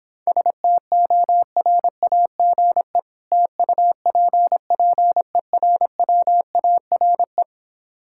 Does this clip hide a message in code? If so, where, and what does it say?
Yes: Morse code from 0.3 to 7.4 s, heard as a keyed tone, about 26 wpm, 700 Hz. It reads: STORAGE TUPPERWARE